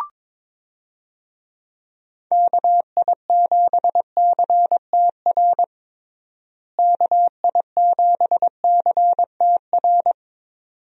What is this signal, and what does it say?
Morse code, heard as a keyed tone, about 22 wpm, 700 Hz. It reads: KI7CTR KI7CTR
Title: USA Call Signs - 6 Characters - Repeated 2x - 22wpm